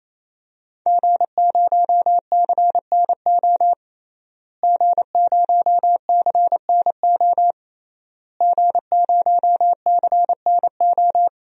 Morse code, heard as a keyed tone, about 28 wpm, 700 Hz. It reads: G0CNO G0CNO G0CNO
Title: DXCC-100 Call Signs - 5 Characters - Repeated 3x - 28wpm